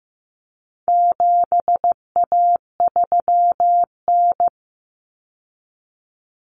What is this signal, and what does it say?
Morse code, heard as a keyed tone, about 15 wpm, 700 Hz. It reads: 7A3N
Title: Common Cut Numbers - 4-Digit 15wpm